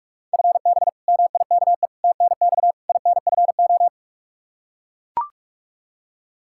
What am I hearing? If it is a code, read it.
Morse code, heard as a keyed tone, about 43 wpm, 700 Hz. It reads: FB MIKE TNX INFO
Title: Practice QSO - Exchange 3 - More Info - 45wpm